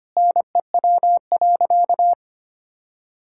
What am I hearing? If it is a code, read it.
Morse code, heard as a keyed tone, about 25 wpm, 700 Hz. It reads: NEW.
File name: practice / 25wpm / Sentences with 2-Word Phrases - 25wpm